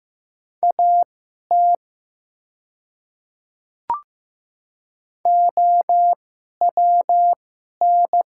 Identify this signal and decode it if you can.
Morse code, heard as a keyed tone, about 15 wpm, spaced as for 11 wpm, 700 Hz. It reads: AT OWN
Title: 2x Character Spacing - Top 300 Words - 15wpm